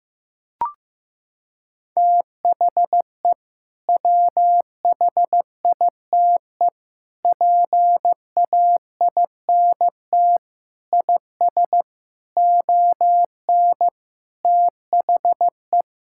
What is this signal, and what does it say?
Morse code, heard as a keyed tone, about 15 wpm, 700 Hz. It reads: THE WHITE PAINT IS ON THE
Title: Sentences from Top 1000 Words - Review Part 6 - 15wpm